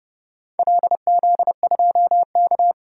Morse code, heard as a keyed tone, about 29 wpm, 700 Hz. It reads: LZ2K
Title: DXCC-100 Call Signs - 4 Characters - 30wpm